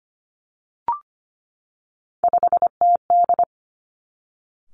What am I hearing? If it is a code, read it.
Morse code, heard as a keyed tone, about 25 wpm, 700 Hz. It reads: 5TD
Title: Three Challenging Letter-Number Combinations 25wpm